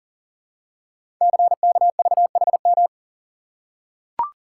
Morse code, heard as a keyed tone, about 41 wpm, 700 Hz. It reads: CKVHM
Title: Code Groups - 5 Letters - 40wpm